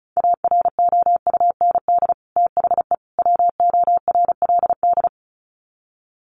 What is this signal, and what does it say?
Morse code, heard as a keyed tone, about 34 wpm, 700 Hz. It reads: AROUND THE WORLD
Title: Sets of 3 Words 35wpm